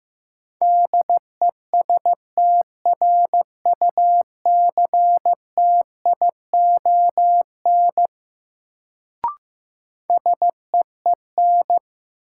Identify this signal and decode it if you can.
Morse code, heard as a keyed tone, about 15 wpm, 700 Hz. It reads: DESTRUCTION SEEN
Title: Sets of 5 Words Encore - New Word Drill - 15wpm